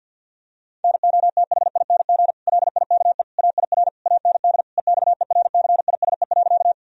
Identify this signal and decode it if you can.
Morse code, heard as a keyed tone, about 50 wpm, 700 Hz. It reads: NOTHING LIKE AIR AND EXERCISE.
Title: Sentences from Top 1000 Words - Review Part 4 - 50wpm